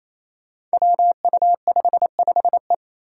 Morse code, heard as a keyed tone, about 29 wpm, 700 Hz. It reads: WU55E
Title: Five Challenging Letter-Number Combinations 28wpm